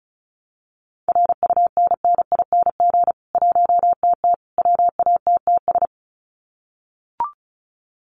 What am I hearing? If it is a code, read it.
Morse code, heard as a keyed tone, about 34 wpm, 700 Hz. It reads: RUNNING 1TT WATTS
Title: Common QSO Phrases 35wpm